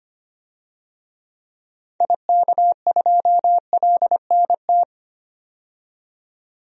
Morse code, heard as a keyed tone, about 25 wpm, 700 Hz. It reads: IK2LNT
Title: DXCC-100 Call Signs - 6 Characters - 25wpm